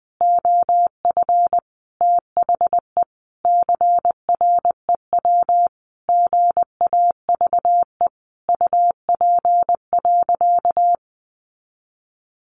Morse code, heard as a keyed tone, about 20 wpm, 700 Hz. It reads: OF THE CREW GAVE UP.
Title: Sentences from Top 1500 Words 20wpm